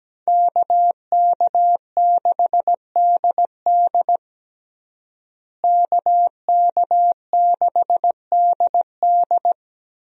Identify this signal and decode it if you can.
Morse code, heard as a keyed tone, about 17 wpm, 700 Hz. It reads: KK6DD KK6DD
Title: USA Call Signs - 5 Characters - Repeated 2x - 17wpm